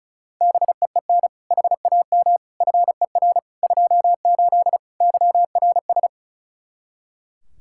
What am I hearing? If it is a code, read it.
Morse code, heard as a keyed tone, about 36 wpm, 700 Hz. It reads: BEEN HAM FER 28 YRS
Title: Practice QSO - Exchange 3 - More Info - 35wpm